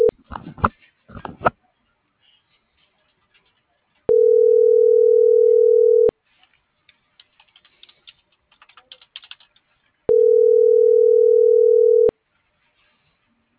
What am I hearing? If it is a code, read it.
Ringback tone (North America), 440 + 480 Hz.
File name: CENTRAL SOTOVA / out alejandro.rodriguez - .54939